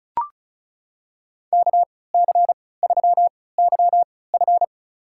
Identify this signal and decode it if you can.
Morse code, heard as a keyed tone, about 36 wpm, spaced as for 20 wpm, 700 Hz. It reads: KC3YF
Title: USA Call Signs - 5 Characters - 3x Character Spacing - 35wpm